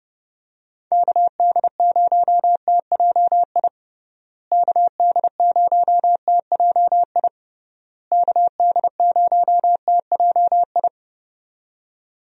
Morse code, heard as a keyed tone, about 30 wpm, 700 Hz. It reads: KD0TJI KD0TJI KD0TJI
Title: USA Call Signs - 6 Characters - Repeated 3x - 30wpm